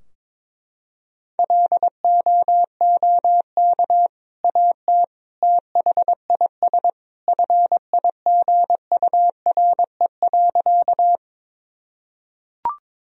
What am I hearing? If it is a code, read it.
Morse code, heard as a keyed tone, about 22 wpm, 700 Hz. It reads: LOOK AT THIS FIGURE.
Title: Sentences from Top 1000 Words - Review Part 2 - 22wpm